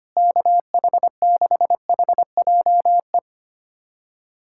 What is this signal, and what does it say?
Morse code, heard as a keyed tone, about 25 wpm, 700 Hz. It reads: KH6HJE
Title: USA Callsign DE Callsign 25wpm